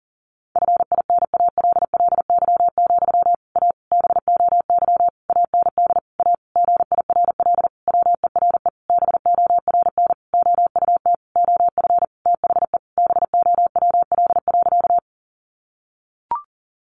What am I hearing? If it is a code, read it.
Morse code, heard as a keyed tone, about 40 wpm, 700 Hz. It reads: FINALLY, A BOY AND A GIRL WERE BORN OUT OF THE BOWL.
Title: Sentences from Top 1200 Words 40wpm